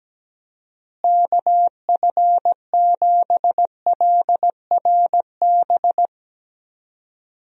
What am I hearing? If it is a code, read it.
Morse code, heard as a keyed tone, about 17 wpm, 700 Hz. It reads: KF7LRB
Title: USA Call Signs - 6 Characters - Repeated 2x - 17wpm